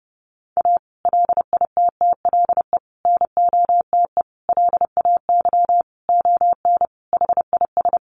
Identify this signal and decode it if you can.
Morse code, heard as a keyed tone, about 30 wpm, 700 Hz. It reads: A LITTLE NOTE LAY ON HIS
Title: Sentences from Top 1500 Words 30wpm